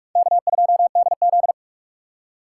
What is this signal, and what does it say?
Morse code, heard as a keyed tone, about 46 wpm, 700 Hz. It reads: K2DZ
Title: USA Call Signs - 4 Characters - 6x Character Spacing - 45wpm